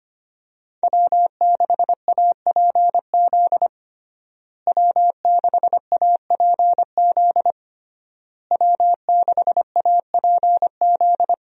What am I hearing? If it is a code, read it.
Morse code, heard as a keyed tone, about 25 wpm, 700 Hz. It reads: W6APZ W6APZ W6APZ
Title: USA Call Signs - 5 Characters - Repeated 3x - 25wpm